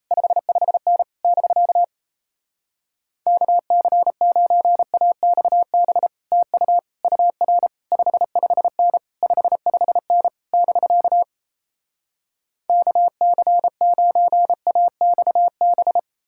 Morse code, heard as a keyed tone, about 38 wpm, 700 Hz. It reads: 55N <BK> KC9AXB TU UR 55N 55N <BK> KC9AXB